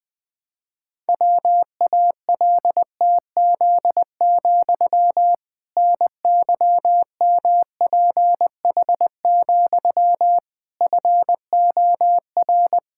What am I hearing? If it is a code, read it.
Morse code, heard as a keyed tone, about 20 wpm, 700 Hz. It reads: WALTZ, NYMPH, FOR
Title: Pangram Challenge v2 - 20wpm